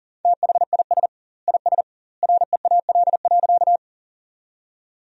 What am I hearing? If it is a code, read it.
Morse code, heard as a keyed tone, about 37 wpm, 700 Hz. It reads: THIS IS REAL.
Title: 2x Word-Spacing - Sentences from Top 300 Words - 40wpm